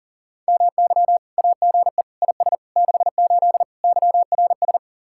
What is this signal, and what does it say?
Morse code, heard as a keyed tone, about 49 wpm, 700 Hz. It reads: MY AGE IS 68 YRS
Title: Speed-Racing Common QSO Phrases - 40wpm